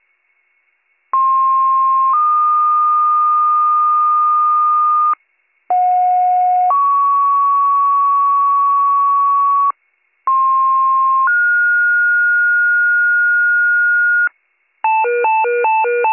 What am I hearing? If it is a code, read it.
Two-tone page, heard as a one-second tone, then a three-second tone, about 1060 then 1220 Hz, about 730 then 1120 Hz, about 1060 then 1430 Hz, with noise. It is heard from 1.1 to 14.3 s.